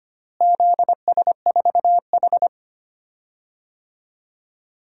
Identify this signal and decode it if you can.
Morse code, heard as a keyed tone, about 25 wpm, 700 Hz. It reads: ZS4H
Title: DXCC-100 Call Signs - 4 Characters - Repeated 3x - 25wpm